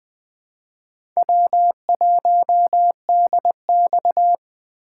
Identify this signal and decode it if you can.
Morse code, heard as a keyed tone, about 20 wpm, 700 Hz. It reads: W1DX
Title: K1USN SST QSO Practice - 20wpm